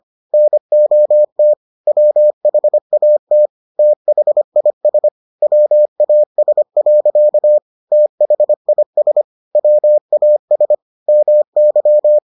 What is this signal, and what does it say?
Morse code, heard as a keyed tone, about 25 wpm, 600 Hz. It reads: NOT WHAT THIS WAS. THIS WAS MY